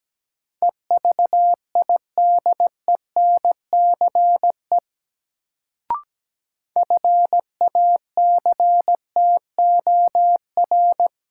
Morse code, heard as a keyed tone, about 17 wpm, 700 Hz. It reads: EVIDENCE FACTOR
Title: Sentences from Top 1300 Words - New Word Drill - 17wpm